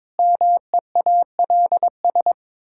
Morse code, heard as a keyed tone, about 22 wpm, 700 Hz. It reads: MEALS
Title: Pangram Challenge v2 - 22wpm